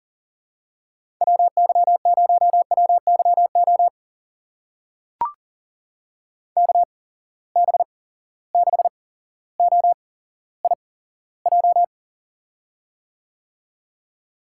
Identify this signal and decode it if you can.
Morse code, heard as a keyed tone, about 41 wpm, spaced as for 17 wpm, 700 Hz. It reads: WY0WYO KB6OIJ
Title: USA Call Signs - 6 Characters - 8x Character Spacing - 40wpm